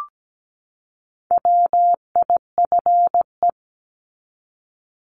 Morse code, heard as a keyed tone, about 17 wpm, 700 Hz. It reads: WIFE
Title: Top 1751-2000 Words 17wpm